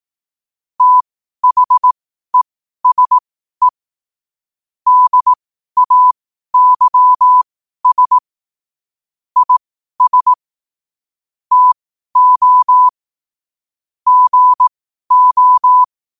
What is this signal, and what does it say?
Morse code, heard as a keyed tone, about 18 wpm, spaced as for 12 wpm, 1000 Hz. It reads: THESE DAYS IS TO GO
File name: qotdfiles / 12wpm